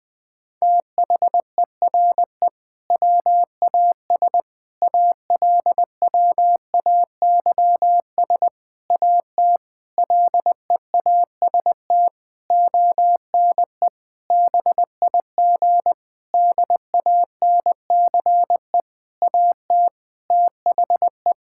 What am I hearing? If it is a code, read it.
Morse code, heard as a keyed tone, about 21 wpm, 700 Hz. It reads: THERE WAS ALWAYS AT LEAST ONE BIG DANCE AT THE
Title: Sentences from Top 1600 Words 20wpm